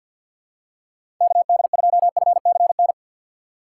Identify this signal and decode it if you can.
Morse code, heard as a keyed tone, about 52 wpm, 700 Hz. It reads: KD2FCN
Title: USA Call Signs - 6 Characters - 6x Character Spacing - 50wpm